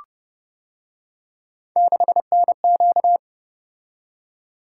Morse code, heard as a keyed tone, about 29 wpm, 700 Hz. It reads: 6NQ